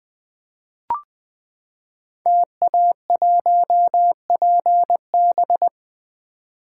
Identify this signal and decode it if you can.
Morse code, heard as a keyed tone, about 20 wpm, 700 Hz. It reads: TA1PB
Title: Code Groups - 5 Letter:Numbers - 20wpm